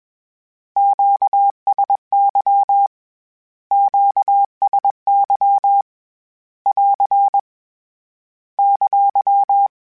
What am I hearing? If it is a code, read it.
Morse code, heard as a keyed tone, about 21 wpm, 800 Hz. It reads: QSY QSY <AR> !
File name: Q Calls / Q Calls-mp3s / Q Calls Twice @20